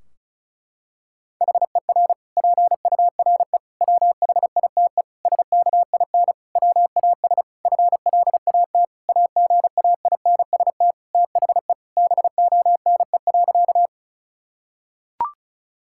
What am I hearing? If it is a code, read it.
Morse code, heard as a keyed tone, about 34 wpm, 700 Hz. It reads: HER PURE WHITE SKIN WAS FLAT AGAINST THE BONE.
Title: Sentences from Top 1600 Words 35wpm